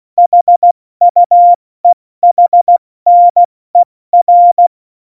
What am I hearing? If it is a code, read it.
Morse code, heard as a keyed tone, about 16 wpm, 700 Hz. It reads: HUEHNER
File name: Morsekurs / Witze / Witz 22. August 17, Tempo 70 F80